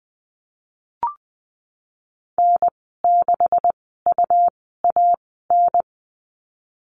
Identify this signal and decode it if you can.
Morse code, heard as a keyed tone, about 20 wpm, spaced as for 14 wpm, 700 Hz. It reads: N6UAN
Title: 161 Lesson 26g - Letter g - Call Signs 20wpm